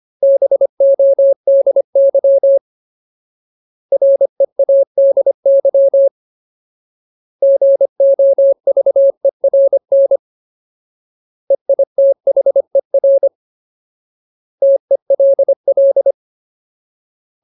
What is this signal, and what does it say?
Morse code, heard as a keyed tone, about 25 wpm, 550 Hz. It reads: BODY READY GOVERN EITHER TELL
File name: Ultimate Word List / 25WPM Ultimate